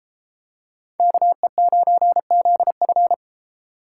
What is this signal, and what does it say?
Morse code, heard as a keyed tone, about 33 wpm, 700 Hz. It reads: KE9ZF